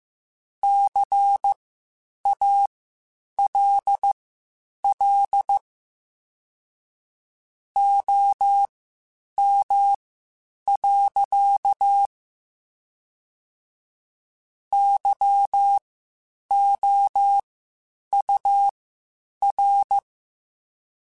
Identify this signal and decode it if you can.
Morse code, heard as a keyed tone, about 15 wpm, spaced as for 8 wpm, 800 Hz. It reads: CALL OM. YOUR